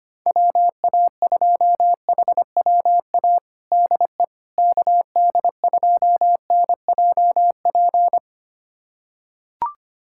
Morse code, heard as a keyed tone, about 25 wpm, 700 Hz. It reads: WA2HWA DE KD2NJP
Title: USA Callsign DE Callsign - 6 Characters - 25wpm